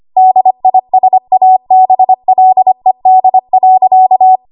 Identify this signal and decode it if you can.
Morse code, heard as a keyed tone, about 25 wpm, 750 Hz. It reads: DISABLED.
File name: Archive / 25 WPM / 25WPM